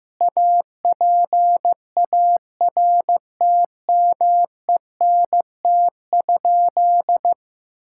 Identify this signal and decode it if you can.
Morse code, heard as a keyed tone, about 15 wpm, 700 Hz. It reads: APARTMENT?
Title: Sentences from Top 1000 Words - Review Part 6 - 15wpm